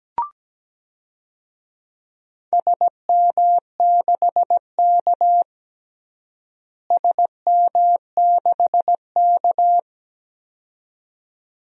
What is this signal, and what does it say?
Morse code, heard as a keyed tone, about 17 wpm, 700 Hz. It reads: SM6K SM6K
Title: DXCC-100 Call Signs - 4 Characters - Repeated 2x - 17wpm